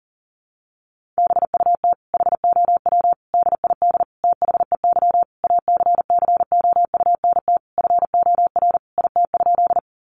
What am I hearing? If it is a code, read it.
Morse code, heard as a keyed tone, about 37 wpm, 700 Hz. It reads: BUT HOW DID THEY ACCOUNT FOR IT?
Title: Sentences from Top 900 Words 40wpm